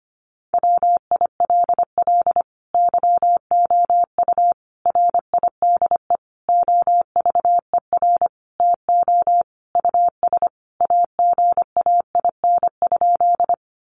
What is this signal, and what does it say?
Morse code, heard as a keyed tone, about 25 wpm, 700 Hz. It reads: WILL YOU RIDE OVER TO US AGAIN?
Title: Sentences from Top 1200 Words 25wpm